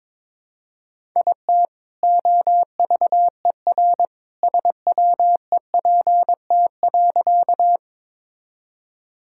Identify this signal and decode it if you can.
Morse code, heard as a keyed tone, about 22 wpm, 700 Hz. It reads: IT OVER SWEPT.
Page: morse-code-ninja-practice-22wpm